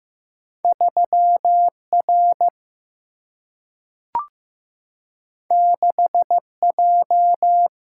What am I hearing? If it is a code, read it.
Morse code, heard as a keyed tone, about 15 wpm, 700 Hz. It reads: 3 R 6 J
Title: Single-Letter Single-Number 15wpm